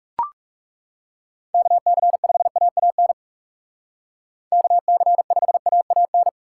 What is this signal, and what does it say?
Morse code, heard as a keyed tone, about 37 wpm, 700 Hz. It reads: KC5AAN KC5AAN
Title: Speed Racing - USA 2x3 Call Signs 30wpm